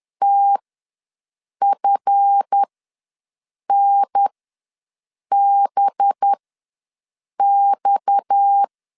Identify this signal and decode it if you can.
Morse code, heard as a keyed tone, about 10 wpm, spaced as for 6 wpm, 800 Hz. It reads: TFNBX
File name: LEZIONI / 7 GF9Z / 7 GF9 ws 10wpm 4034